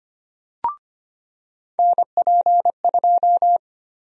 Morse code, heard as a keyed tone, about 25 wpm, 700 Hz. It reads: NP2
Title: USA Call Sign Prefixes - 25wpm